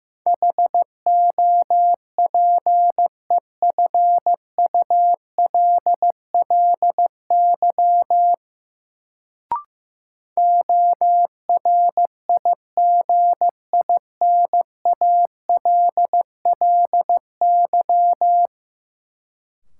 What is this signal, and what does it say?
Morse code, heard as a keyed tone, about 15 wpm, 700 Hz. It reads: HOPEFULLY ORIGINALLY
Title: Sentences from Top 1500 Words - New Word Drill - 15wpm